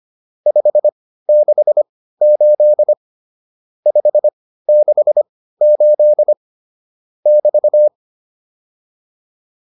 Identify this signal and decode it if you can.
Morse code, heard as a keyed tone, about 25 wpm, spaced as for 15 wpm, 600 Hz. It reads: 568 568 <BT>